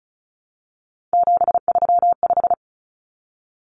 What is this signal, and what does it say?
Morse code, heard as a keyed tone, about 34 wpm, 700 Hz. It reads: 735
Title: Common Cut Numbers - 3-Digit 35wpm